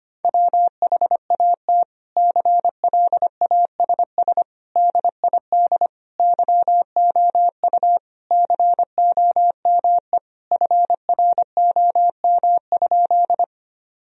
Morse code, heard as a keyed tone, about 25 wpm, 700 Hz. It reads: WHAT CLASS DID YOU COME FROM?
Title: Sentences from Top 1000 Words - Review Part 2 - 25wpm